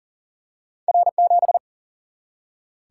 Morse code, heard as a keyed tone, about 39 wpm, 700 Hz. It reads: R7
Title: Single-Letter Single-Number 40wpm